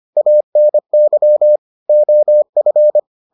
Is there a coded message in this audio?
Yes: Morse code from 0.2 to 3.0 s, heard as a keyed tone, about 25 wpm, 600 Hz. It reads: ANY OF